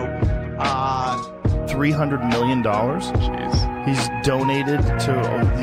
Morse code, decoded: none